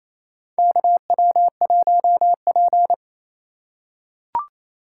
Morse code, heard as a keyed tone, about 28 wpm, 700 Hz. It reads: KW1P
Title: USA Call Signs - 4 Characters - 4x Character Spacing - 28wpm